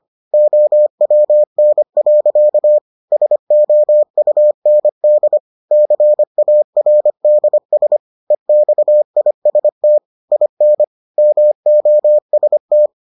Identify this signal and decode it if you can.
Morse code, heard as a keyed tone, about 25 wpm, 600 Hz. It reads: OWN. SOUND CARDS EXIST IN MOST